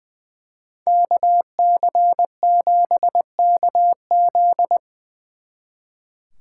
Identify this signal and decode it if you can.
Morse code, heard as a keyed tone, about 20 wpm, 700 Hz. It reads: KC7KZ
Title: Speed Racing - USA Call Signs 20wpm